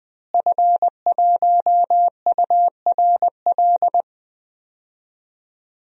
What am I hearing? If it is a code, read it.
Morse code, heard as a keyed tone, about 21 wpm, 700 Hz. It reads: F1URL
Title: DXCC-100 Call Signs - 5 Characters - Repeated 3x - 20wpm